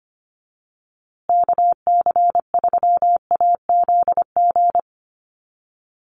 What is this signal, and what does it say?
Morse code, heard as a keyed tone, about 25 wpm, 700 Hz. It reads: KC3AZG
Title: USA Call Signs Repeated 2x 25wpm